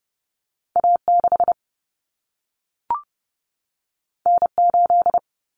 Morse code, heard as a keyed tone, about 30 wpm, 700 Hz. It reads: A6 N8
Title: Common Cut Numbers - 2-Digit 30wpm